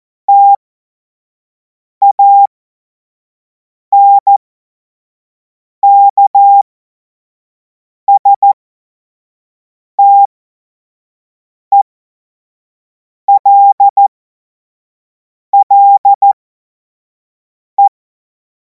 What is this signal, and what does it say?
Morse code, heard as a keyed tone, about 14 wpm, spaced as for 5 wpm, 800 Hz. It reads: TANKSTELLE
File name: Morsekurs / Witze / Witz 23. Juni 2026 25 F70